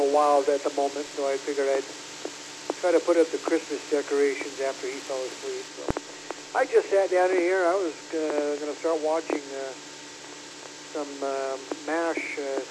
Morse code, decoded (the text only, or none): none